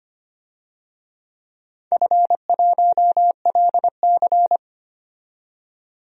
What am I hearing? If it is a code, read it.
Morse code, heard as a keyed tone, about 25 wpm, 700 Hz. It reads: F1LC
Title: DXCC-100 Call Signs - 4 Characters - 25wpm